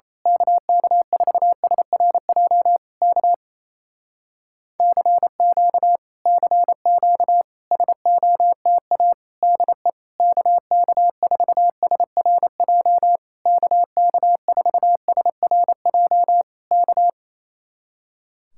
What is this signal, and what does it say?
Morse code, heard as a keyed tone, about 27 wpm, 700 Hz. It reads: KK4SRJ K CQ CQ SOTA DE KK4SRJ KK4SRJ K
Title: Speed-Racing Common QSO Phrases Encore - 28wpm